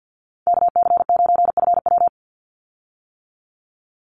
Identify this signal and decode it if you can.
Morse code, heard as a keyed tone, about 50 wpm, 700 Hz. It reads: KC9FW
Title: USA Call Signs 50wpm